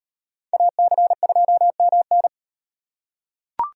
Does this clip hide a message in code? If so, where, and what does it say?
Yes: Morse code from 0.5 to 2.3 s, heard as a keyed tone, about 36 wpm, 700 Hz. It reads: AC2MN